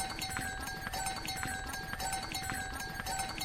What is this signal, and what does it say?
Emergency Alert System (SAME) tones, heard as harsh buzzing bursts. It is heard from 0.4 to 3.0 s.
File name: wrct-log